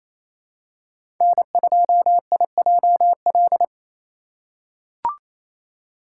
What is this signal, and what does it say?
Morse code, heard as a keyed tone, about 27 wpm, 700 Hz. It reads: N2IJL
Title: USA Call Signs - 5 Characters - 2x Character Spacing - 28wpm